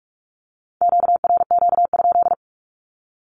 Morse code, heard as a keyed tone, about 46 wpm, 700 Hz. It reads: QRQ?
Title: Q-Codes with Spoken Meaning - 45wpm